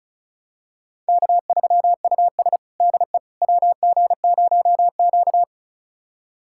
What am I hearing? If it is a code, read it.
Morse code, heard as a keyed tone, about 36 wpm, 700 Hz. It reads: K3US DE WG0Q
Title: USA Callsign DE Callsign - 4 Characters - 35wpm